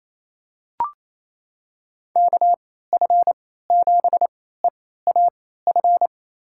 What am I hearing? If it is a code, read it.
Morse code, heard as a keyed tone, about 28 wpm, spaced as for 16 wpm, 700 Hz. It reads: KF7EAF
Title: USA Call Signs - 3x Character Spacing - 28wpm